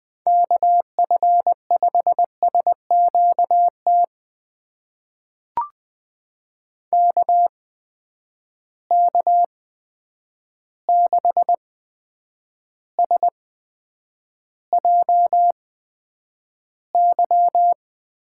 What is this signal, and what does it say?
Morse code, heard as a keyed tone, about 20 wpm, spaced as for 9 wpm, 700 Hz. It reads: KF5SQT KK6SJY
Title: USA Call Signs - 6 Characters - 8x Character Spacing - 20wpm